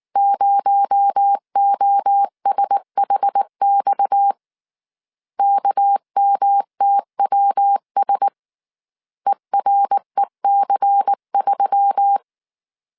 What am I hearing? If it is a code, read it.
Morse code, heard as a keyed tone, about 19 wpm, 800 Hz. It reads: 0OSHX KMTWS EREC3
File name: CW / DET / 1 6 1500C S C 18wpm 2356